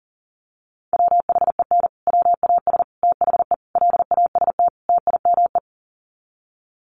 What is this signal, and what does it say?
Morse code, heard as a keyed tone, about 40 wpm, 700 Hz. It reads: WHEN WAS THE LAST TIME
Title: Sets of 5 Words Encore 40wpm